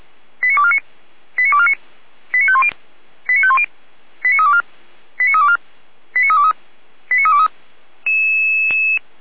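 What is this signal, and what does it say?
Five-tone selective call (ZVEI), heard as quick runs of five tones, tones of about 70 ms. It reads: 89249 89249 89510 89510 89324 89324 89323 89323
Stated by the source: ZVEI five-tone selective call 89337 received on an SDR, from the Feuerwehr Orlamünde audio archive.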